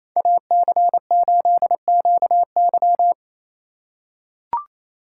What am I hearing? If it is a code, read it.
Morse code, heard as a keyed tone, about 28 wpm, 700 Hz. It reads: AC8QY
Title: USA Call Signs - 2x Character Spacing - 28wpm